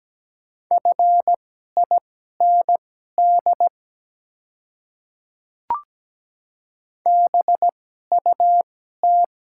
Morse code, heard as a keyed tone, about 17 wpm, spaced as for 12 wpm, 700 Hz. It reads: FIND BUT
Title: 2x Character Spacing - Top 200 Words - 17wpm